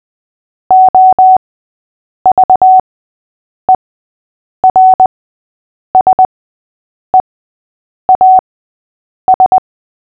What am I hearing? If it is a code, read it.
Morse code, heard as a keyed tone, about 20 wpm, spaced as for 8 wpm, 750 Hz. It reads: OVERSEAS
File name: cw / fists / Short-Story-3-8wpm